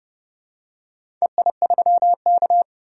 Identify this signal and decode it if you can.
Morse code, heard as a keyed tone, about 30 wpm, 700 Hz. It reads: EI3K
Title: DXCC-100 Call Signs - 4 Characters - 30wpm